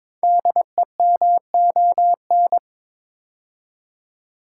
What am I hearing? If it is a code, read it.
Morse code, heard as a keyed tone, about 22 wpm, 700 Hz. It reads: DEMON
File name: 5 Letter Words - First 3 Letters Given - 22wpm